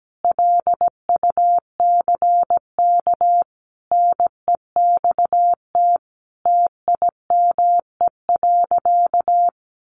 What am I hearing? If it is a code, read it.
Morse code, heard as a keyed tone, about 17 wpm, 700 Hz. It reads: LUCK NEXT TIME.
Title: Sentences from Top 1500 Words 17wpm